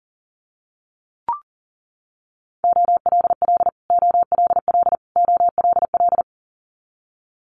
Morse code, heard as a keyed tone, about 40 wpm, 700 Hz. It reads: OLL OLL OLL
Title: Multi-letter Phonemes Repeated 3x 40wpm